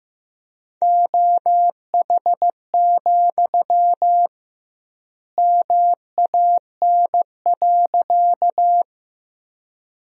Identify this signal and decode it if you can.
Morse code, heard as a keyed tone, about 15 wpm, 700 Hz. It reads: OH, MAN.
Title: 2x Word-Spacing - Sentences from Top 500 Words - 15wpm